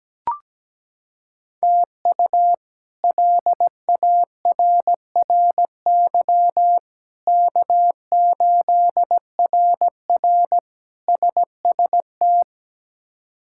Morse code, heard as a keyed tone, about 17 wpm, 700 Hz. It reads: TU LARRY K8RR SST
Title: K1USN SST QSO Practice - 17wpm